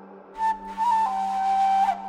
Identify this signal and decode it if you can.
There is no code.